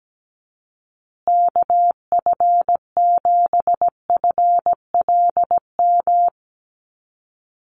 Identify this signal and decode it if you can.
Morse code, heard as a keyed tone, about 17 wpm, 700 Hz. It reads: KF7FLM
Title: USA Call Signs Repeated 3x 17wpm